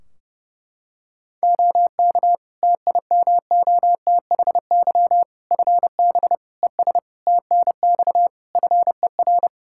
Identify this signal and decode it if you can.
Morse code, heard as a keyed tone, about 30 wpm, 700 Hz. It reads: OK TIMOTHY FB ES TNX FER
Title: Practice QSO - Exchange 2 - Info - 30wpm